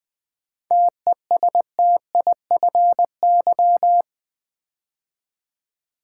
Morse code, heard as a keyed tone, about 20 wpm, 700 Hz. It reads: TESTIFY
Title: 7 Letter Words - First 3 Letters Given - 20wpm